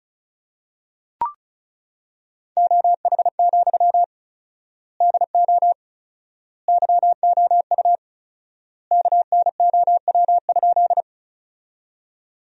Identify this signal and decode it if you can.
Morse code, heard as a keyed tone, about 36 wpm, 700 Hz. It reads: OH, DO YOU KNOW?
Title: 4x Word-Spacing - Sentences from Top 500 Words - 35wpm